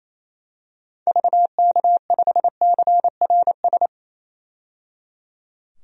Morse code, heard as a keyed tone, about 28 wpm, 700 Hz. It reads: VK5CRS
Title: DXCC-100 Call Signs - 6 Characters - 28wpm